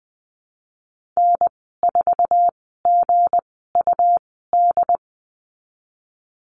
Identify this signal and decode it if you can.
Morse code, heard as a keyed tone, about 20 wpm, spaced as for 15 wpm, 700 Hz. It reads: N4GUD
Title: 161 Lesson 26g - Letter g - Call Signs 20wpm